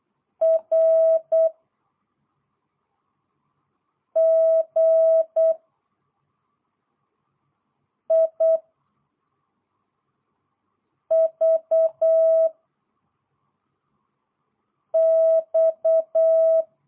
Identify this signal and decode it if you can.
Morse code, heard as a keyed tone, about 8 wpm, 660 Hz. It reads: R G I V X